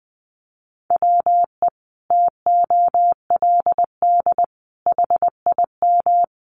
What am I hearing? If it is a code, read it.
Morse code, heard as a keyed tone, about 20 wpm, 700 Hz. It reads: WE TOLD HIM
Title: Sentences from Top 1300 Words 20wpm